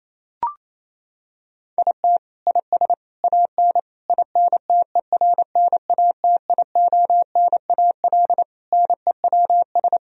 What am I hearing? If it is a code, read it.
Morse code, heard as a keyed tone, about 28 wpm, 700 Hz. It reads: IT IS AN INTERNATIONAL NEWS
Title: Sentences from Top 1000 Words - Review Part 4 - 28wpm